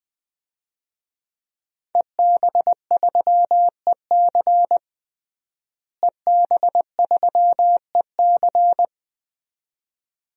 Morse code, heard as a keyed tone, about 20 wpm, 700 Hz. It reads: EB3EC EB3EC
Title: DXCC-100 Call Signs - 5 Characters - Repeated 2x - 20wpm